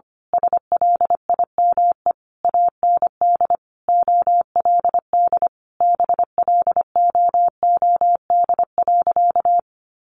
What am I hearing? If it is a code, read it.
Morse code, heard as a keyed tone, about 25 wpm, 700 Hz. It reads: SLIME AND OLD BLOOD.
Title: A Sound Of Thunder 25wpm